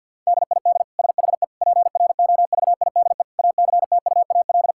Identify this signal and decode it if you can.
Morse code, heard as a keyed tone, about 50 wpm, 700 Hz. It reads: DID SHE PROVIDE ACTUAL
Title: Sentences from Top 1000 Words - Review Part 5 - 50wpm